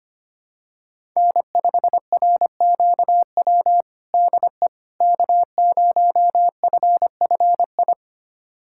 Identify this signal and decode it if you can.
Morse code, heard as a keyed tone, about 25 wpm, 700 Hz. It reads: N5RQW DE K0FFI
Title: USA Callsign DE Callsign 25wpm